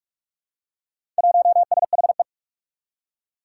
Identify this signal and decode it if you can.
Morse code, heard as a keyed tone, about 42 wpm, 700 Hz. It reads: 1SHE
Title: Four Challenging Letter-Number Combinations 45wpm